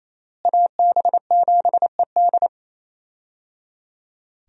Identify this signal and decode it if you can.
Morse code, heard as a keyed tone, about 28 wpm, 700 Hz. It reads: AB7ED
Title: USA Call Signs - 5 Characters - Repeated 3x - 28wpm